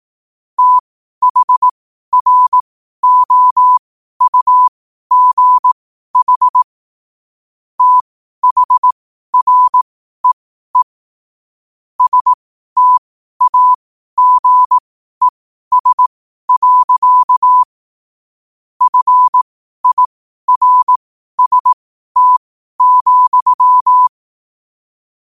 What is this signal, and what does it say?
Morse code, heard as a keyed tone, about 18 wpm, spaced as for 12 wpm, 1000 Hz. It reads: THROUGH THREE STAGES. FIRST,
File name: qotdfiles / 12wpm